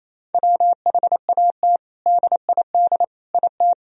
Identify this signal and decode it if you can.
Morse code, heard as a keyed tone, about 28 wpm, 700 Hz. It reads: WHAT DID IT
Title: Sentences from Top 1000 Words - Review Part 2 - 28wpm